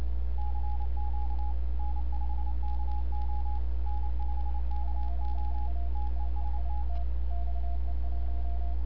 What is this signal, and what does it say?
Morse code, heard as a keyed tone, about 28 wpm, 830 Hz, with radio noise. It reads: CQ N5KO N5KO TEST
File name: n6tr / XE1RCS 1025UTC